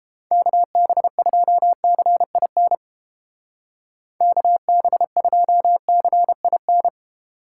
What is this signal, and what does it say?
Morse code, heard as a keyed tone, about 29 wpm, 700 Hz. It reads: KB2CIN KB2CIN